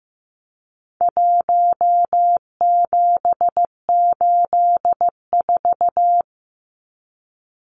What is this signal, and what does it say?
Morse code, heard as a keyed tone, about 15 wpm, 700 Hz. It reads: 1784
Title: Numbers - 4-Digit 15wpm